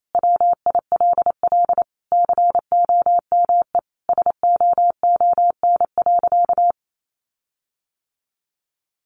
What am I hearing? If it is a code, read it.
Morse code, heard as a keyed tone, about 28 wpm, 700 Hz. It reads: WILL COME SOON.
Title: Sentences from Top 1500 Words 28wpm